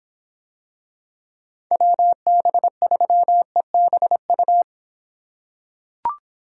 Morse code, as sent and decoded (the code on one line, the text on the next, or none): .-- -... ...-- . -... ..-
WB3EBU